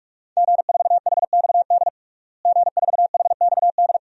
Morse code, heard as a keyed tone, about 41 wpm, 700 Hz. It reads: G4HXD G4HXD
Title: DXCC-100 Call Signs - 5 Characters - Repeated 2x - 45wpm